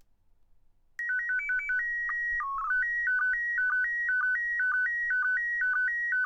SSTV (slow-scan television), PD 120.